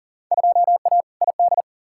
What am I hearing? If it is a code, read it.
Morse code, heard as a keyed tone, about 40 wpm, 700 Hz. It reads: 2A ID
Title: Field Day Exchanges Encore - 40wpm